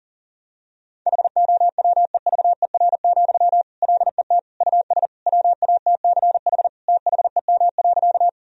Morse code, heard as a keyed tone, about 40 wpm, 700 Hz. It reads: HOWEVER, LET US WATCH THEM.
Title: Sentences from Top 1000 Words - Review Part 1 - 40wpm